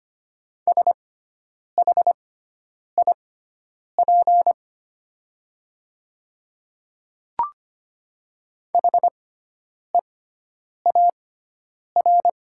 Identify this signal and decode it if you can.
Morse code, heard as a keyed tone, about 25 wpm, spaced as for 9 wpm, 690 Hz. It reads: SHIP HEAR
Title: 6x Character Spacing - Top 500 Words - 25wpm